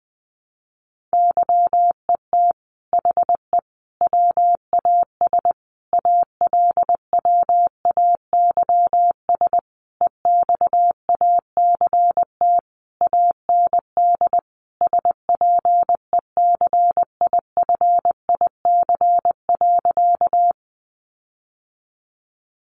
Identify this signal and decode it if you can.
Morse code, heard as a keyed tone, about 19 wpm, 700 Hz. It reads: YET HE WAS ALWAYS EXACT AND SPECIFIC.
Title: Sentences from Top 1300 Words 20wpm